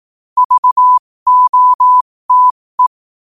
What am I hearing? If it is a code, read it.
Morse code, heard as a keyed tone, about 18 wpm, spaced as for 15 wpm, 1000 Hz. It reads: VOTE
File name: qotdfiles / 15wpm